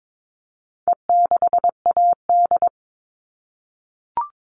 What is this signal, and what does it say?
Morse code, heard as a keyed tone, about 22 wpm, 700 Hz. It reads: E6AD
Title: All Cut Numbers - 4-Digit 22wpm